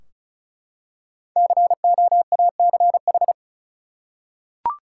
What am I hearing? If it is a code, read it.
Morse code, heard as a keyed tone, about 36 wpm, 700 Hz. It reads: COACH
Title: Five Letter Words - 35wpm